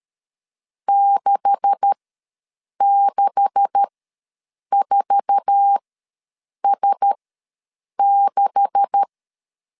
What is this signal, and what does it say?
Morse code, heard as a keyed tone, about 13 wpm, spaced as for 7 wpm, 800 Hz. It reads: 664S6
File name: LEZIONI / 9 286 ws 12wpm 3351